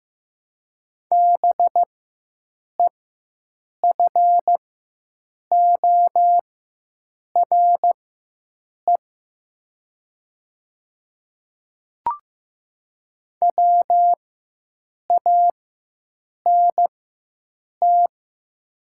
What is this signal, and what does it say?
Morse code, heard as a keyed tone, about 15 wpm, spaced as for 7 wpm, 700 Hz. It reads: BEFORE WANT
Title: 4x Character Spacing - Top 300 Words - 15wpm